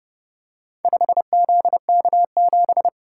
Morse code, heard as a keyed tone, about 30 wpm, 700 Hz. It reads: 5ZK7